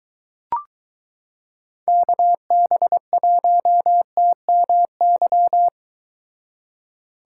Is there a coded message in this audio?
Yes: Morse code from 1.9 to 5.7 s, heard as a keyed tone, about 23 wpm, 700 Hz. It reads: KB1TMY